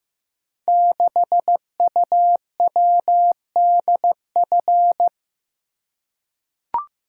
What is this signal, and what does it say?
Morse code, heard as a keyed tone, about 15 wpm, 700 Hz. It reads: 6UWDF
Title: Code Groups - 5 Letter:Numbers - 15wpm